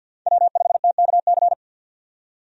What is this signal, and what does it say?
Morse code, heard as a keyed tone, about 51 wpm, 700 Hz. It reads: W5TKC
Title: USA Call Signs - 5 Characters - 50wpm